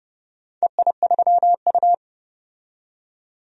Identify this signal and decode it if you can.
Morse code, heard as a keyed tone, about 32 wpm, 700 Hz. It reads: EI3U
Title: DXCC-100 Call Signs - 4 Characters - 30wpm